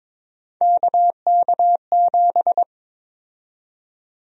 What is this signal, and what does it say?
Morse code, heard as a keyed tone, about 22 wpm, 700 Hz. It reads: KK7
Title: USA Call Sign Prefixes - Repeated 2x - 22wpm